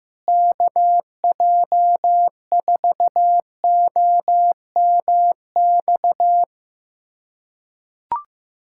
Morse code, heard as a keyed tone, about 15 wpm, 700 Hz. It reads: KJ4OMX
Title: USA Callsign DE Callsign - 6 Characters - 15wpm